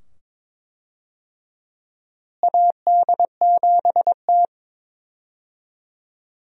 Morse code, heard as a keyed tone, about 22 wpm, 700 Hz. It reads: AD7T
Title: USA Call Signs - 4 Characters - Repeated 2x - 22wpm